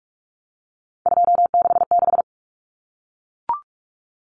Morse code, heard as a keyed tone, about 44 wpm, 700 Hz. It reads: 266
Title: Common Cut Numbers - 3-Digit 45wpm